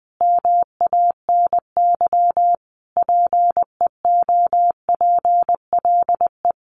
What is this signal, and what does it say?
Morse code, heard as a keyed tone, about 20 wpm, 700 Hz. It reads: MANY PEOPLE
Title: Sentences from Top 1100 Words 20wpm